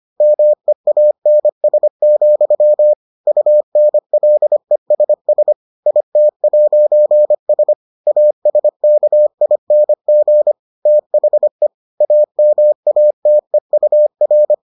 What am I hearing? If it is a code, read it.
Morse code, heard as a keyed tone, about 25 wpm, 600 Hz. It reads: MEANS, UNLESS IT'S ASKING THE AMATEUR